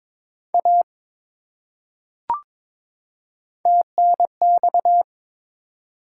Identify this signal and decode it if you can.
Morse code, heard as a keyed tone, about 22 wpm, 700 Hz. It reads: A TNX
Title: Top 180 QSO Elements 22wpm